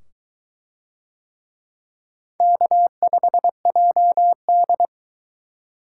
Morse code, heard as a keyed tone, about 22 wpm, 700 Hz. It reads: K5JD